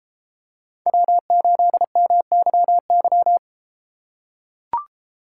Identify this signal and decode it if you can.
Morse code, heard as a keyed tone, about 32 wpm, 700 Hz. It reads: W8MYY